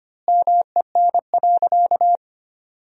Morse code, heard as a keyed tone, about 25 wpm, 700 Hz. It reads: MEN.
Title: 2x Word-Spacing - Sentences from Top 300 Words - 25wpm